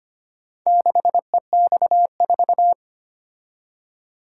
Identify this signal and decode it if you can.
Morse code, heard as a keyed tone, about 25 wpm, 700 Hz. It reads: 6EX4
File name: practice / 25wpm / Four Letter-Number Combinations - 25wpm